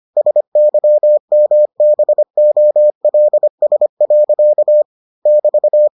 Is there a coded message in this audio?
Yes: Morse code from 0.2 to 5.9 s, heard as a keyed tone, about 25 wpm, 600 Hz. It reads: SYMBOLS. <BT>